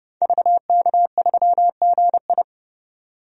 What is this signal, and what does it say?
Morse code, heard as a keyed tone, about 30 wpm, 700 Hz. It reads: VK3GI
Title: DXCC-100 Call Signs - 5 Characters - 30wpm